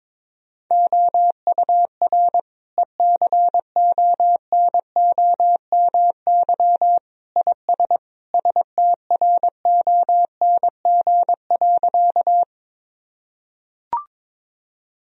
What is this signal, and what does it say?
Morse code, heard as a keyed tone, about 22 wpm, 700 Hz. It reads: OUR ECONOMY IS STRONG.
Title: Sentences from Top 1000 Words - Review Part 3 - 22wpm